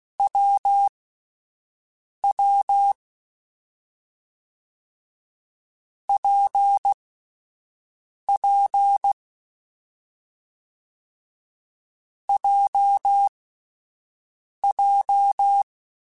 Morse code, heard as a keyed tone, about 16 wpm, spaced as for 5 wpm, 800 Hz. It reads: WW PP JJ
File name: Difficult Combos / Difficult Combos-mp3s / WPJ WPJ1 @5